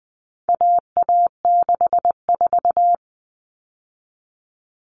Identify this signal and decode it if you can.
Morse code, heard as a keyed tone, about 20 wpm, 700 Hz. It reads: AA64
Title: All Cut Numbers - 4-Digit 20wpm